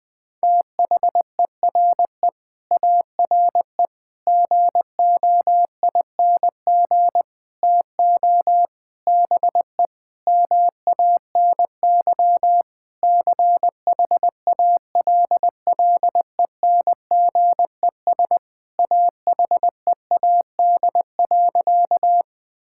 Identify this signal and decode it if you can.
Morse code, heard as a keyed tone, about 20 wpm, 700 Hz. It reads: THERE ARE GOING TO BE MANY CHALLENGES AHEAD.
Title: Sentences with 5-Word Phrases - Spoken Beginning - 20wpm